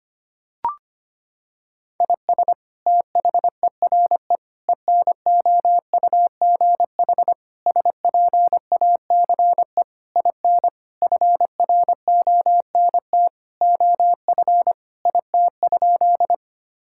Morse code, heard as a keyed tone, about 25 wpm, 700 Hz. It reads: IS THERE ENOUGH SPACE IN FRONT OF IT?
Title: Sentences from Top 1000 Words - Review Part 5 - 25wpm